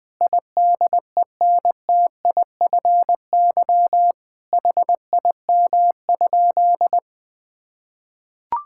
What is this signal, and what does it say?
Morse code, heard as a keyed tone, about 21 wpm, 700 Hz. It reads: IDENTIFY HIM?
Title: Sentences from Top 1000 Words - Review Part 6 - 20wpm